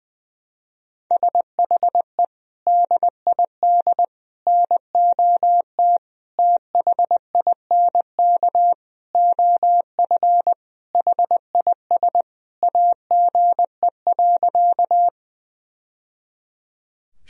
Morse code, heard as a keyed tone, about 20 wpm, 700 Hz. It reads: SHE DID NOT THINK OF HIS AGE.
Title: Sentences from Top 1000 Words - Review Part 3 - 20wpm